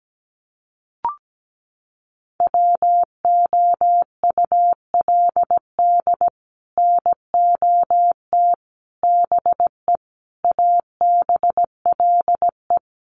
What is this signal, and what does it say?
Morse code, heard as a keyed tone, about 17 wpm, 700 Hz. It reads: WOULD NOT BE ABLE
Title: Sets of 5 Words 17wpm